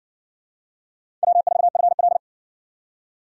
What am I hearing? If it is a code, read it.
Morse code, heard as a keyed tone, about 55 wpm, 700 Hz. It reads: W4FL